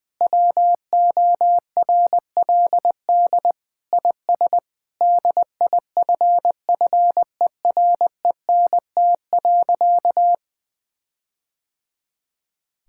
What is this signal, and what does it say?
Morse code, heard as a keyed tone, about 19 wpm, 700 Hz. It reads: WORLD IS DIFFERENT.
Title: Sentences with 4-Word Phrases - 20wpm